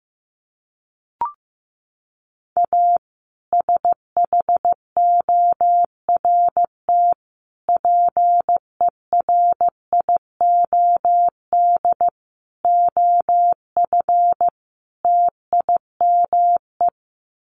Morse code, heard as a keyed tone, about 14 wpm, 700 Hz. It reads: A SHORT PERIOD OF TIME
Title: Sets of 5 Words Encore 15wpm